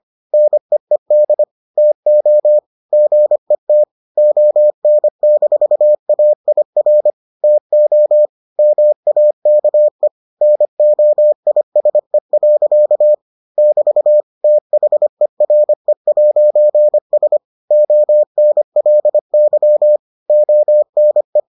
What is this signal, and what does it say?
Morse code, heard as a keyed tone, about 25 wpm, 600 Hz. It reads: NEED TO GET ON-AIR TO MAKE NOISE. <BT> THERE'S ONLY ONE